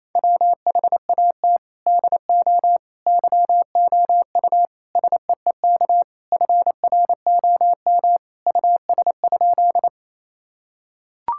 Morse code, heard as a keyed tone, about 28 wpm, 700 Hz. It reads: WHAT DO YOU SEEK FROM US?
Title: Sentences from Top 1600 Words 28wpm